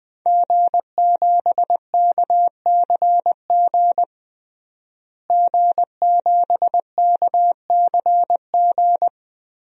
Morse code, heard as a keyed tone, about 20 wpm, 700 Hz. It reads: G7KCG G7KCG
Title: DXCC-100 Call Signs - 5 Characters - Repeated 3x - 20wpm